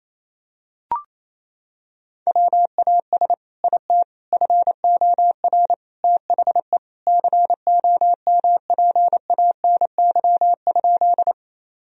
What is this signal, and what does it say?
Morse code, heard as a keyed tone, about 28 wpm, 700 Hz. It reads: WAS IT FOR THE COMPANY?
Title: Sentences from Top 1000 Words - Review Part 3 - 28wpm